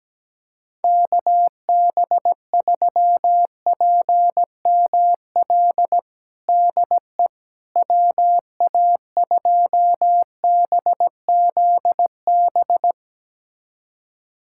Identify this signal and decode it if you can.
Morse code, heard as a keyed tone, about 17 wpm, 700 Hz. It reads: KB3PML DE WA2BZB
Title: USA Callsign DE Callsign - 6 Characters - 17wpm